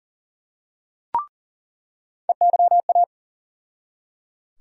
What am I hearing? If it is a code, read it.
Morse code, heard as a keyed tone, about 38 wpm, 700 Hz. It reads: EYA
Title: USA Call Sign Postfixes - 3 Letters 40wpm